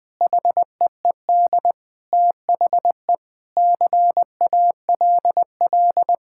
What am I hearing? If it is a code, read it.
Morse code, heard as a keyed tone, about 20 wpm, 700 Hz. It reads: HEED THE CALL
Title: Sets of 8 Words - 20wpm